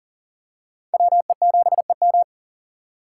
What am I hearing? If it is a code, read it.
Morse code, heard as a keyed tone, about 40 wpm, 700 Hz. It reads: WE7EM
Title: USA Call Signs - 5 Characters - 6x Character Spacing - 40wpm